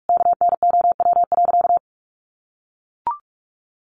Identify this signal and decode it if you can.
Morse code, heard as a keyed tone, about 44 wpm, 700 Hz. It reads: KNOW.
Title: Sentences from Top 1200 Words 45wpm